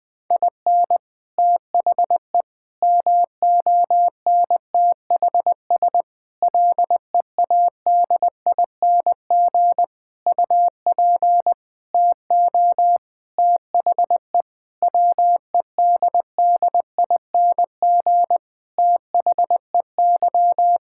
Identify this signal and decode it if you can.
Morse code, heard as a keyed tone, about 20 wpm, 700 Hz. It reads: IN THE MONTHS LEADING UP TO THE WEDDING THEY
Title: Sentences with 6-Word Phrases - 20wpm